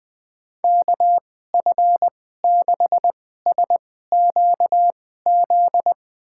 Morse code, heard as a keyed tone, about 21 wpm, spaced as for 15 wpm, 700 Hz. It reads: KF6SQZ
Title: USA Call Signs - 6 Characters - 2x Character Spacing - 20wpm